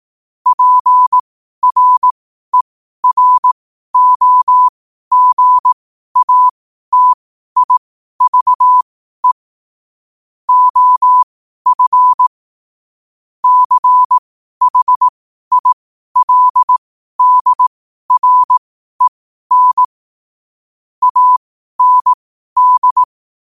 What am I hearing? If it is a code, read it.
Morse code, heard as a keyed tone, about 18 wpm, spaced as for 12 wpm, 1000 Hz. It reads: PREROGATIVE OF CHILDREN AND